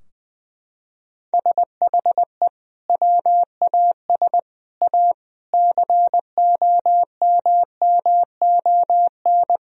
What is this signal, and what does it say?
Morse code, heard as a keyed tone, about 20 wpm, 700 Hz. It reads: SHE WAS A COMMON
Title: Sentences from Top 1000 Words - Review Part 4 - 20wpm